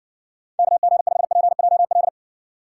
Morse code, heard as a keyed tone, about 70 wpm, 700 Hz. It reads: KG5PJL